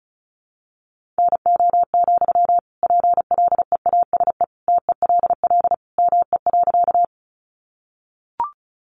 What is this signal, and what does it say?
Morse code, heard as a keyed tone, about 34 wpm, 700 Hz. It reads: NO, PLEASE TELL ME.
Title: Sentences from Top 1500 Words 35wpm